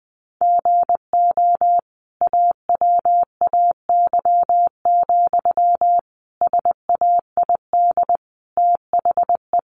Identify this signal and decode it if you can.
Morse code, heard as a keyed tone, about 20 wpm, 700 Hz. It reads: GO AWAY, SAID THE
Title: Sentences from Top 1500 Words 20wpm